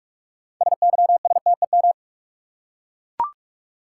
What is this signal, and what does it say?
Morse code, heard as a keyed tone, about 42 wpm, 700 Hz. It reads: SYSTEM
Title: Six Letter Words - 45wpm